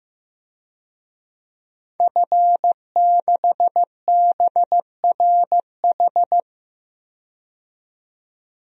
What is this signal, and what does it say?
Morse code, heard as a keyed tone, about 15 wpm, 700 Hz. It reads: F6BRH
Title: DXCC-100 Call Signs - 5 Characters - 15wpm